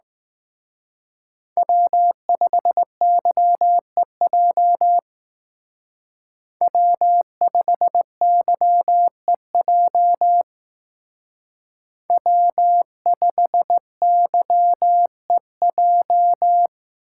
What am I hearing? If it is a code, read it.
Morse code, heard as a keyed tone, about 19 wpm, 700 Hz. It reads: W5YEJ W5YEJ W5YEJ
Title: Speed Racing - USA 1x3 Call Signs 15wpm